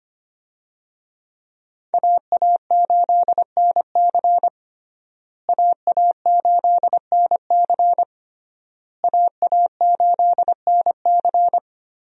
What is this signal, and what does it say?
Morse code, heard as a keyed tone, about 25 wpm, 700 Hz. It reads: AA8NC AA8NC AA8NC
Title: USA Call Signs - 5 Characters - Repeated 3x - 25wpm